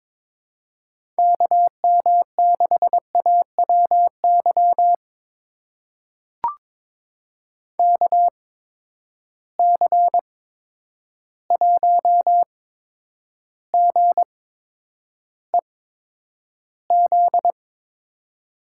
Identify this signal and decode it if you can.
Morse code, heard as a keyed tone, about 22 wpm, spaced as for 9 wpm, 700 Hz. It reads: KM6AWY KC1GEZ
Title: USA Call Signs - 6 Characters - 8x Character Spacing - 22wpm